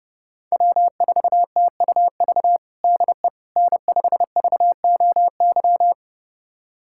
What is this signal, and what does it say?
Morse code, heard as a keyed tone, about 30 wpm, 700 Hz. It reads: W4TUV DE N5VOY
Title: USA Callsign DE Callsign - 5 Characters - 30wpm